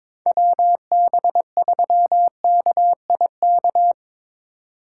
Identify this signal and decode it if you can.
Morse code, heard as a keyed tone, about 22 wpm, 700 Hz. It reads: WB3KIK
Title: USA Call Signs - 6 Characters - 2x Character Spacing - 22wpm